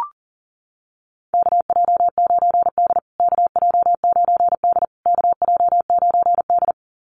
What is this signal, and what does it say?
Morse code, heard as a keyed tone, about 40 wpm, 700 Hz. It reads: KJ9D KJ9D KJ9D
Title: USA Call Signs Repeated 3x 40wpm